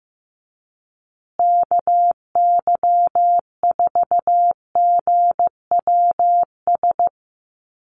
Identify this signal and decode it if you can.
Morse code, heard as a keyed tone, about 15 wpm, 700 Hz. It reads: KY4GWS
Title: USA Call Signs Repeated 3x 15wpm